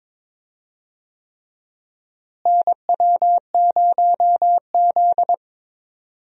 Morse code, heard as a keyed tone, about 22 wpm, 700 Hz. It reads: NW0Z
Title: USA Call Signs - 4 Characters - Repeated 3x - 22wpm